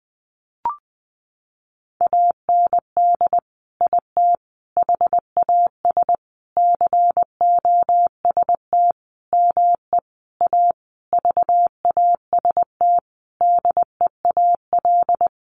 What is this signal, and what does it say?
Morse code, heard as a keyed tone, about 19 wpm, 700 Hz. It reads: AND IT HAS COST ME A VAST DEAL
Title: Sentences from Top 1300 Words 20wpm